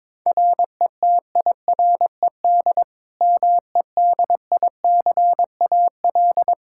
Morse code, heard as a keyed tone, about 22 wpm, 700 Hz. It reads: RETIRED MEDICAL
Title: Practice QSO - Exchange 3 - More Info - 22wpm